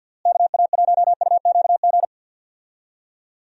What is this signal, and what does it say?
Morse code, heard as a keyed tone, about 50 wpm, 700 Hz. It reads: KA1UQG
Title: USA Call Signs - 6 Characters - Repeated 3x - 50wpm